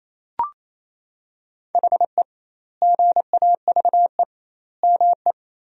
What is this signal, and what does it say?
Morse code, heard as a keyed tone, about 28 wpm, 700 Hz. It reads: HE GAVE ME
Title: 2x Word-Spacing - Sentences from Top 500 Words - 28wpm